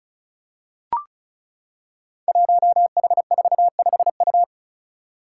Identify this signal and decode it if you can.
Morse code, heard as a keyed tone, about 36 wpm, 700 Hz. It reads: 1H45U
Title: Five Challenging Letter-Number Combinations 35wpm